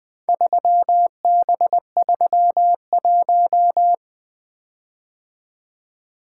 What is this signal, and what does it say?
Morse code, heard as a keyed tone, about 21 wpm, 700 Hz. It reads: 3B31
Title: Four Letter-Number Combinations - 20wpm